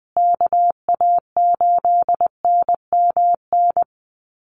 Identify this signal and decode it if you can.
Morse code, heard as a keyed tone, about 20 wpm, 700 Hz. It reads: KA8NMN
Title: USA Call Signs Repeated 3x 20wpm